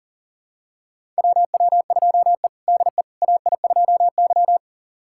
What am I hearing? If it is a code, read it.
Morse code, heard as a keyed tone, about 40 wpm, 700 Hz. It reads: WW2E DE AI2Y
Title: USA Callsign DE Callsign - 4 Characters - 40wpm